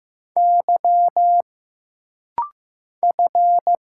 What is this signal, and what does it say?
Morse code, heard as a keyed tone, about 15 wpm, 700 Hz. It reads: Y F
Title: Single Letters - Rapid-Fire 15wpm